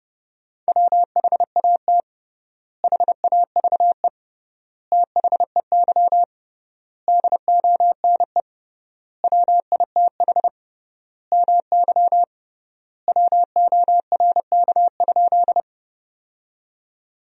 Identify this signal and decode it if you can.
Morse code, heard as a keyed tone, about 30 wpm, 700 Hz. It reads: WHAT HAVE THEY DONE WITH MY WORK?
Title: 3x Word-Spacing - Sentences from Top 500 Words - 30wpm